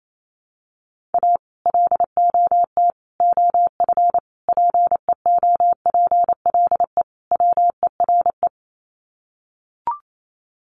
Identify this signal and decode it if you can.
Morse code, heard as a keyed tone, about 28 wpm, 700 Hz. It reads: A LOT OF PEOPLE WERE
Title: Sets of 5 Words Encore2 28wpm